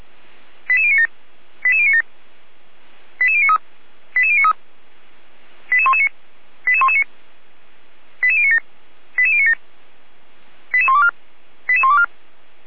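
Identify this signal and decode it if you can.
Five-tone selective call (ZVEI), heard as quick runs of five tones, tones of about 70 ms. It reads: 88097 88097 88093 88093 88119 88119 88097 88097 88124 88124
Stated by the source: ZVEI five-tone selective call received by SDR.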